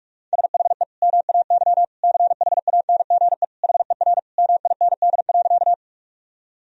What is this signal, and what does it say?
Morse code, heard as a keyed tone, about 41 wpm, 700 Hz. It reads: SHE MAY CHANGE HER MIND.
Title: Sentences from Top 1000 Words - Review Part 1 - 45wpm